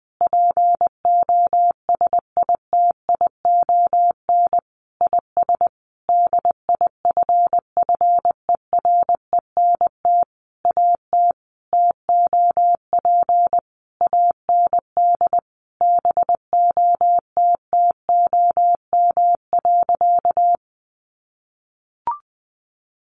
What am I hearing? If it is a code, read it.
Morse code, heard as a keyed tone, about 20 wpm, 700 Hz. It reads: POSITION IS DIFFERENT AT TOP AND BOTTOM.
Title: Sentences from Top 1400 Words 20wpm